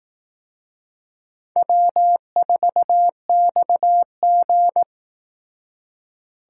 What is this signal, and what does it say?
Morse code, heard as a keyed tone, about 18 wpm, 700 Hz. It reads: W4XG
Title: Speed Racing - USA 1x2 Call Signs 15wpm